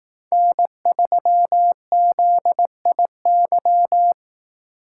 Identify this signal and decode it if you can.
Morse code, heard as a keyed tone, about 18 wpm, 700 Hz. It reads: N3ZIY
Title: Speed Racing - USA 1x3 Call Signs 15wpm